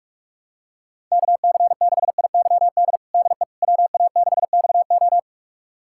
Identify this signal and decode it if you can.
Morse code, heard as a keyed tone, about 41 wpm, 700 Hz. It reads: KC6IYD DE WA6XO
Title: USA Callsign DE Callsign 45wpm